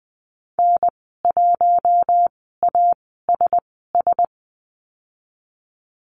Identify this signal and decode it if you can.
Morse code, heard as a keyed tone, about 20 wpm, spaced as for 15 wpm, 700 Hz. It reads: N1ASS
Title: 048 Lesson 9g - Number 4 - Call Signs 20wpm